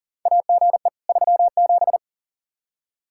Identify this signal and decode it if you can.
Morse code, heard as a keyed tone, about 40 wpm, 700 Hz. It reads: AGE 37
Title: Practice QSO - Exchange 3 - More Info - 40wpm